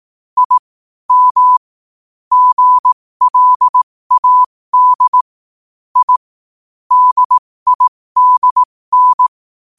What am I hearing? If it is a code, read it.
Morse code, heard as a keyed tone, about 18 wpm, spaced as for 15 wpm, 1000 Hz. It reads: I M GLAD I DIDN
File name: qotdfiles / 15wpm